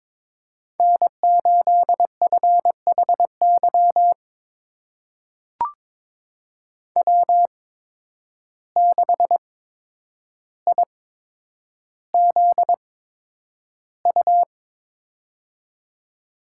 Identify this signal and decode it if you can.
Morse code, heard as a keyed tone, about 22 wpm, spaced as for 9 wpm, 700 Hz. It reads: N8FHY W6IZU
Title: USA Call Signs - 5 Characters - 8x Character Spacing - 22wpm